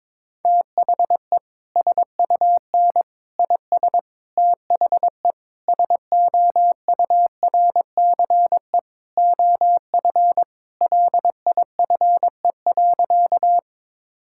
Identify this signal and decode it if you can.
Morse code, heard as a keyed tone, about 22 wpm, 700 Hz. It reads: THE SUN IS THE SOURCE OF LIFE.
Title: Sentences from Top 1000 Words - Review Part 5 - 22wpm